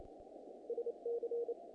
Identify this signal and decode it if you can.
Morse code, heard as a keyed tone, about 28 wpm, 470 Hz. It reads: SC
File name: contests / naqp-cw-k5tr-op-k5pi- -1800